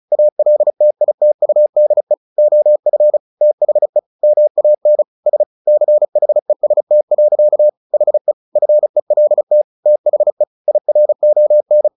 Morse code, heard as a keyed tone, about 34 wpm, 600 Hz. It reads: ALTITUDE OF THE MAN S CHEST. HE FELT THE IRON
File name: the old man and the sea 2 35wpm